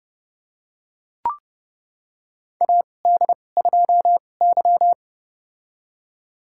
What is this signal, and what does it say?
Morse code, heard as a keyed tone, about 30 wpm, spaced as for 22 wpm, 700 Hz. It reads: AD2Y
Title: USA Call Signs - 4 Characters - 2x Character Spacing - 30wpm